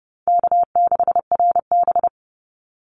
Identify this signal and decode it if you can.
Morse code, heard as a keyed tone, about 30 wpm, 700 Hz. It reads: K6RB
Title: Easy Call Signs - 30wpm